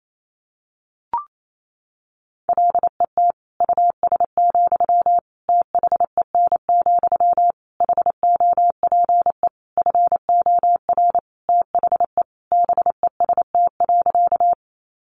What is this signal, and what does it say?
Morse code, heard as a keyed tone, about 28 wpm, 700 Hz. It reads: LET US, THEN, HOPE FOR THE BEST.
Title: Sentences from Top 1000 Words 28wpm